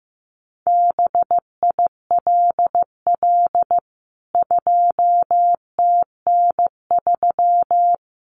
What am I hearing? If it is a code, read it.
Morse code, heard as a keyed tone, about 15 wpm, 700 Hz. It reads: BILL 2TN3
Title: CWT Exchange 15wpm